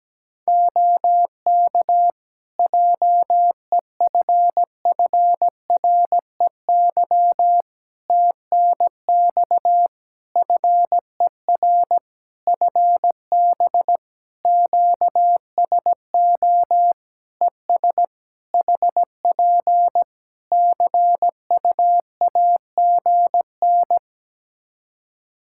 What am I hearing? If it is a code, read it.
Morse code, heard as a keyed tone, about 17 wpm, 700 Hz. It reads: OK JEFFREY TNX FER FB QSO ES HP CUAGN
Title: Practice QSO - Exchange 4 - Ending - 17wpm